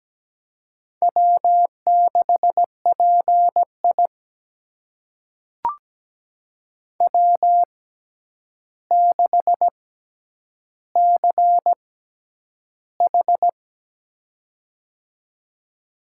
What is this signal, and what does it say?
Morse code, heard as a keyed tone, about 17 wpm, spaced as for 9 wpm, 700 Hz. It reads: W6PI W6CH
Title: USA Call Signs - 4 Characters - 6x Character Spacing - 17wpm